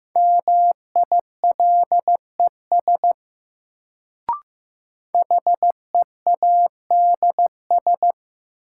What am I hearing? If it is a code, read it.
Morse code, heard as a keyed tone, about 15 wpm, 700 Hz. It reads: MILES HEADS
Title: Sentences from Top 1000 Words - Review Part 3 - New Word Drill - 15wpm